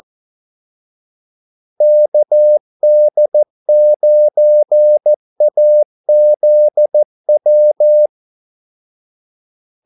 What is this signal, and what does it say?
Morse code, heard as a keyed tone, about 14 wpm, 600 Hz. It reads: KD9AZW
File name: Callsigns-14wpm